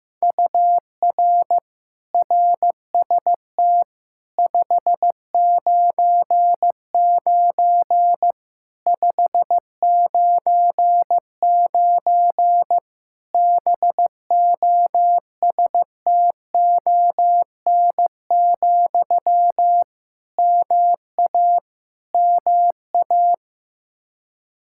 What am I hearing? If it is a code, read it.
Morse code, heard as a keyed tone, about 15 wpm, 700 Hz. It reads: UR RST 599 599 BOSTON, MA MA
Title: Practice QSO - Exchange 1 - Beginning - 15wpm